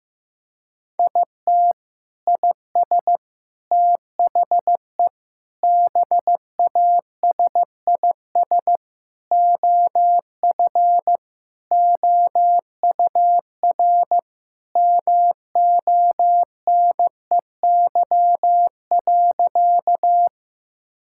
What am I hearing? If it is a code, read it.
Morse code, heard as a keyed tone, about 15 wpm, 700 Hz. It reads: IT IS THE BASIS OF OUR MONEY.
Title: Sentences from Top 1000 Words - Review Part 5 - 15wpm